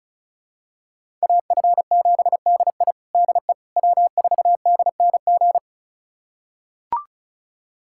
Morse code, heard as a keyed tone, about 34 wpm, 700 Hz. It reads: AF7DI DE W4DNG
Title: USA Callsign DE Callsign - 5 Characters - 35wpm